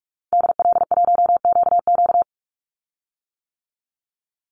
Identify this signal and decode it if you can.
Morse code, heard as a keyed tone, about 44 wpm, 700 Hz. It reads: DL1QQ
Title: Call Signs - 45wpm